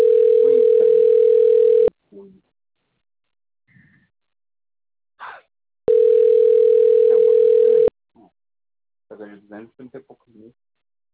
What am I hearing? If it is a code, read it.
Ringback tone (North America), 440 + 480 Hz, with speech.
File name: rg-1600-unknown- - - .306